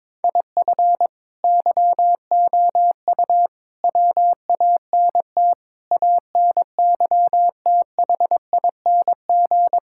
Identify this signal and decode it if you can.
Morse code, heard as a keyed tone, about 22 wpm, 700 Hz. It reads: IF YOU WANT ANYTHING
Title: Sentences from Top 1000 Words - Review Part 6 - 22wpm